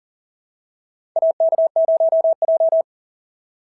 Morse code, heard as a keyed tone, about 40 wpm, 690 Hz. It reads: AK0J
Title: SOTA QSO Practice 40wpm